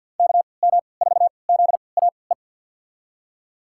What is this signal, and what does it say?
Morse code, heard as a keyed tone, about 47 wpm, spaced as for 28 wpm, 700 Hz. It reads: KM4ZAE
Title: USA Call Signs - 3x Character Spacing - 50wpm